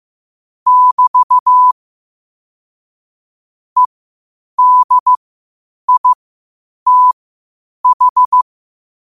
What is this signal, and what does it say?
Morse code, heard as a keyed tone, about 15 wpm, spaced as for 8 wpm, 1000 Hz. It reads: <BT> EDITH
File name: qotdfiles / 8wpm